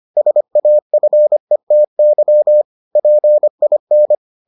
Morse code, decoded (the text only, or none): SAFETY PIN